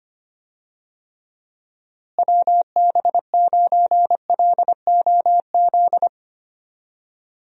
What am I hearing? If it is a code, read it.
Morse code, heard as a keyed tone, about 25 wpm, 700 Hz. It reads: WB9LOZ
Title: USA Call Signs - 6 Characters - Repeated 2x - 25wpm